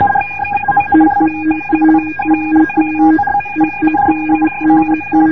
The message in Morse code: DE GKA GBA